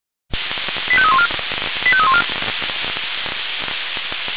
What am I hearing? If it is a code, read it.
Five-tone selective call (ZVEI), heard as quick runs of five tones, tones of about 70 ms. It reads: 85314 85314